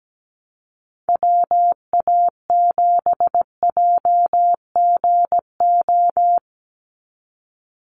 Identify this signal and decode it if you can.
Morse code, heard as a keyed tone, about 17 wpm, 700 Hz. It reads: WA7JGO
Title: USA Call Signs 17wpm